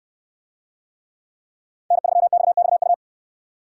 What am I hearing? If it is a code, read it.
Morse code, heard as a keyed tone, about 69 wpm, 700 Hz. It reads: N3XXU